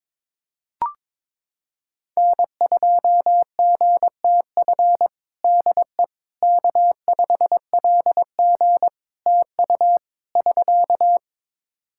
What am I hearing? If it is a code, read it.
Morse code, heard as a keyed tone, about 22 wpm, 700 Hz. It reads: N2GTF DE K5LG TU <SK>
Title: Practice QSO - Exchange 4 - Ending - 22wpm